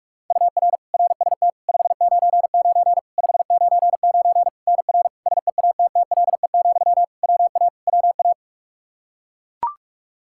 Morse code, heard as a keyed tone, about 41 wpm, 700 Hz. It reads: UR RST 599 599 NR SEATTLE, WA WA
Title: Practice QSO - Exchange 1 - Beginning - 45wpm